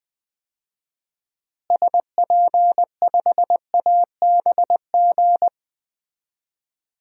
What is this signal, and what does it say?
Morse code, heard as a keyed tone, about 20 wpm, 700 Hz. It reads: SP5ABG
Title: DXCC-100 Call Signs - 6 Characters - Repeated 2x - 20wpm